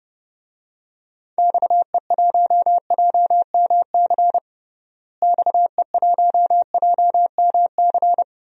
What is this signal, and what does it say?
Morse code, heard as a keyed tone, about 32 wpm, 700 Hz. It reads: XE1JMC XE1JMC
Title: DXCC-100 Call Signs - 6 Characters - Repeated 2x - 30wpm